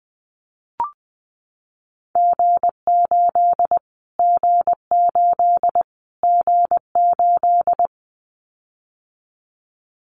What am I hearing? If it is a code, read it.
Morse code, heard as a keyed tone, about 20 wpm, 700 Hz. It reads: G8 G8 G8
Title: Callsign Prefixes for Top 25 DXCC - Repeated 3x - 20wpm